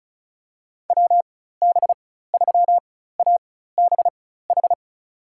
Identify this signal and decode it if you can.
Morse code, heard as a keyed tone, about 34 wpm, spaced as for 16 wpm, 700 Hz. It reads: WB3ABH
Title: USA Call Signs - 6 Characters - 4x Character Spacing - 35wpm